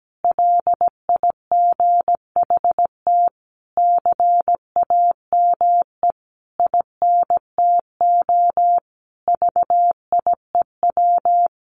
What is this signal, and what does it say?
Morse code, heard as a keyed tone, about 17 wpm, 700 Hz. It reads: LIGHT CAME INTO VIEW
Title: Sentences from Top 1100 Words 17wpm